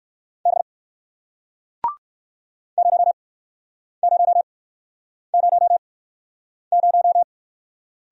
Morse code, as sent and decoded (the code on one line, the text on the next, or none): -... / ----- ----- ----- -----
B 0000